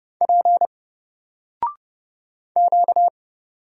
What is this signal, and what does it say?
Morse code, heard as a keyed tone, about 30 wpm, 700 Hz. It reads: P Q
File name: Single-Letter-Number-30wpm